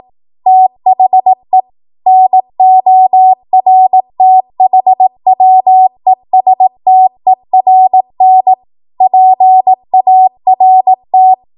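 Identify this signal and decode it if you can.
Morse code, heard as a keyed tone, about 18 wpm, 750 Hz. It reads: THE NORTHWESTERN PART
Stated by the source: ARRL 18 wpm Morse code practice archive transmission.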